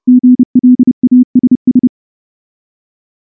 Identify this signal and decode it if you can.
Morse code, heard as a keyed tone, about 28 wpm, 250 Hz. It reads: GLASS
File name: morse / morse-training-30WPM